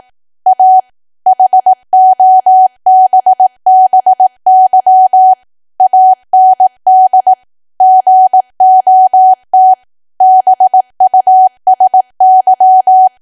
Morse code, heard as a keyed tone, about 18 wpm, 750 Hz. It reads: A HOBBY AND GOT BUSY
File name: Archive / 18 WPM / 18WPM